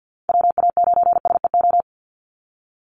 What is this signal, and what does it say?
Morse code, heard as a keyed tone, about 50 wpm, 700 Hz. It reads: WA9SEO
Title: Call Signs Repeated 3x - 50wpm